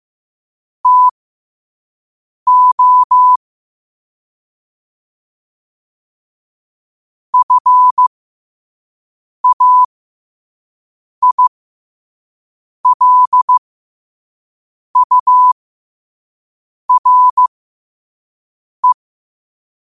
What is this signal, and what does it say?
Morse code, heard as a keyed tone, about 15 wpm, spaced as for 5 wpm, 1000 Hz. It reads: TO FAILURE
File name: qotdfiles / 5wpm